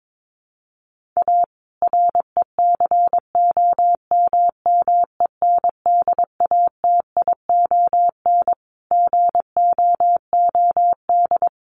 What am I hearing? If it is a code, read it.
Morse code, heard as a keyed tone, about 22 wpm, 700 Hz. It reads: A RECOMMENDATION GOOD
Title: Sentences from Top 1400 Words 22wpm